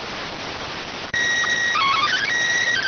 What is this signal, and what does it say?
SSTV (slow-scan television), PD 290.